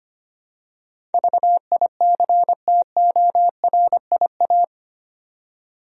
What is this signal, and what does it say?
Morse code, heard as a keyed tone, about 25 wpm, 700 Hz. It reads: VICTORIA
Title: USA Common Names - 25wpm